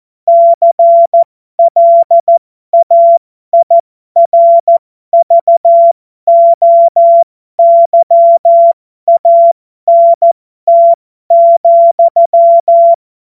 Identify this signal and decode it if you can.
Morse code, heard as a keyed tone, about 14 wpm, spaced as for 12 wpm, 680 Hz. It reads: CLAIRVOYANT,
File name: Morsekurs / Witze